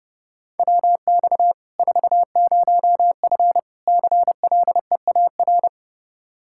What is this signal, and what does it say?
Morse code, heard as a keyed tone, about 30 wpm, 700 Hz. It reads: WX 40F CLEAR
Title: Practice QSO - Exchange 2 - Info - 30wpm